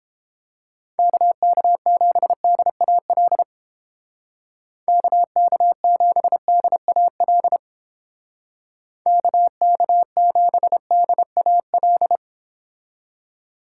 Morse code, heard as a keyed tone, about 29 wpm, 700 Hz. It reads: KK7DAL KK7DAL KK7DAL